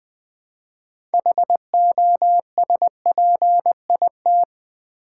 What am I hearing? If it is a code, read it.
Morse code, heard as a keyed tone, about 20 wpm, 700 Hz. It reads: HOSPIT
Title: Build A Word - 2 Letters At A Time - 8-Letter Words - 20wpm